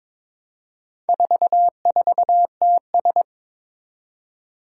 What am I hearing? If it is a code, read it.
Morse code, heard as a keyed tone, about 22 wpm, 700 Hz. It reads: 44TS
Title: Four Challenging Letter-Number Combinations 22wpm